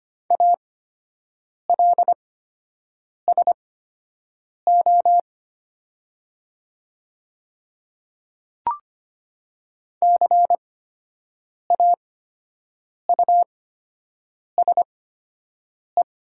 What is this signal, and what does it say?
Morse code, heard as a keyed tone, about 25 wpm, spaced as for 7 wpm, 700 Hz. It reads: ALSO CAUSE
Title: 8x Character Spacing - Top 200 Words - 25wpm